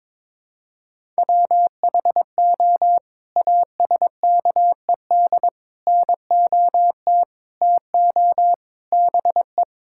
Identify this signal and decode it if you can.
Morse code, heard as a keyed tone, about 22 wpm, 700 Hz. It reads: WHO ASKED NOT TO BE
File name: practice / 22wpm / Sets of 6 Words - 22wpm